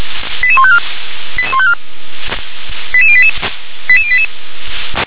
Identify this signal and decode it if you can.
Five-tone selective call (ZVEI), heard as quick runs of five tones, tones of about 70 ms. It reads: 88154 88154 88088 88088